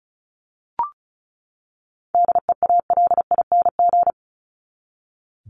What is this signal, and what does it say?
Morse code, heard as a keyed tone, about 34 wpm, 700 Hz. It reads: DEALING
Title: ING Suffix - Top 100 Words with Base Words 35wpm